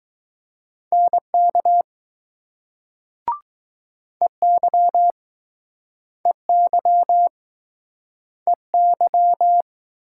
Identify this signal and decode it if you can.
Morse code, heard as a keyed tone, about 20 wpm, 700 Hz. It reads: NK EY EY EY